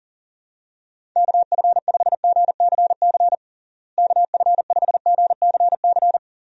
Morse code, heard as a keyed tone, about 40 wpm, 700 Hz. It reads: KF5GCC KF5GCC
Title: USA Call Signs - 6 Characters - Repeated 2x - 40wpm